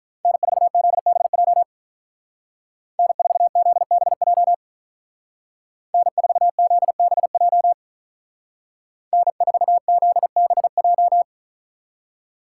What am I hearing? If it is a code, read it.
Morse code, heard as a keyed tone, about 42 wpm, 700 Hz. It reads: N4ZBJ N4ZBJ N4ZBJ N4ZBJ
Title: Speed Racing - USA 1x3 Call Signs 35wpm